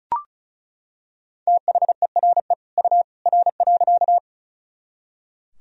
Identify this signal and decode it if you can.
Morse code, heard as a keyed tone, about 36 wpm, 700 Hz. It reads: THERE U R.
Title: Sentences from Top 100 Words and Abbreviations 35wpm